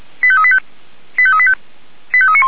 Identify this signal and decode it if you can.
Five-tone selective call (ZVEI), heard as quick runs of five tones, tones of about 70 ms. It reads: 85385 85385 85391
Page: /464/SDR/ZVEI/